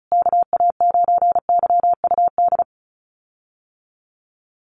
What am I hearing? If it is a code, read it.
Morse code, heard as a keyed tone, about 34 wpm, 700 Hz. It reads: KA9YUD
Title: USA Call Signs Repeated 3x 35wpm